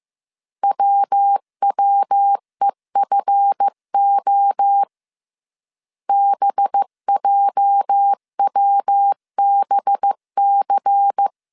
Morse code, heard as a keyed tone, about 15 wpm, 800 Hz. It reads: WWEFO BJWBC
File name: CW / DET / 1 9 1500C S A 14wpm 3222